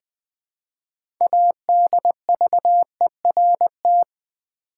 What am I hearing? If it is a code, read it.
Morse code, heard as a keyed tone, about 20 wpm, 700 Hz. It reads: ADVERT
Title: Build A Word - 3 Letters At A Time - 9-Letter Words - 20wpm